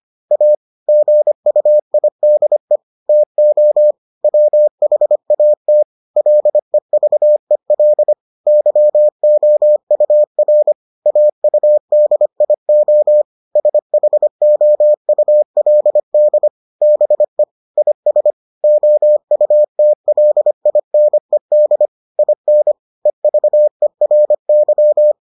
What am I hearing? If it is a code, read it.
Morse code, heard as a keyed tone, about 25 wpm, 600 Hz. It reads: A GUIDE TO WHAT LEVEL YOUR AUDIO SHOULD BE IS OUTLINED IN EVERY